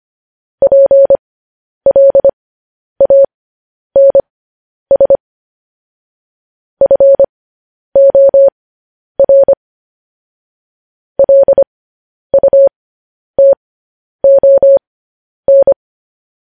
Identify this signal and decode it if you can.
Morse code, heard as a keyed tone, about 25 wpm, spaced as for 10 wpm, 550 Hz. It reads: PLANS FOR LUTON
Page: news headlines in morse code at 10 wpm